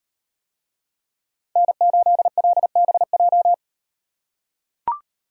Morse code, heard as a keyed tone, about 37 wpm, 700 Hz. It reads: N8LBJ